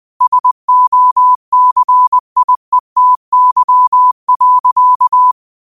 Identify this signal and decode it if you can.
Morse code, heard as a keyed tone, about 20 wpm, 1000 Hz. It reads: SOCIETY.
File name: qotdfiles / 20wpm